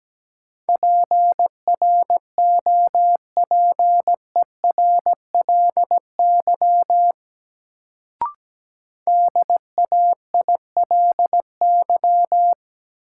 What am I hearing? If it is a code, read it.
Morse code, heard as a keyed tone, about 17 wpm, 700 Hz. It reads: PROPERLY DAILY
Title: Sentences from Top 1100 Words - New Word Drill - 17wpm